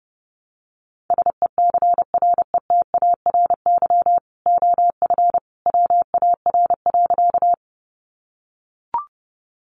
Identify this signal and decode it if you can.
Morse code, heard as a keyed tone, about 30 wpm, 700 Hz. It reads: SECRETARY OF WAR.
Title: Sentences from Top 1300 Words 30wpm